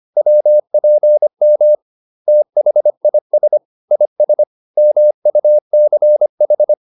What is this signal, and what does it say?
Morse code, heard as a keyed tone, about 25 wpm, 600 Hz. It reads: WPM THIS IS MUCH